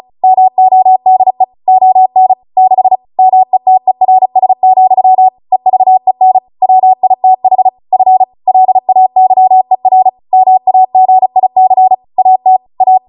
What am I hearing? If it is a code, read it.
Morse code, heard as a keyed tone, about 36 wpm, 750 Hz. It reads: MODE ON 6 METERS, EVEN WITH F LAYER MAGIC AT A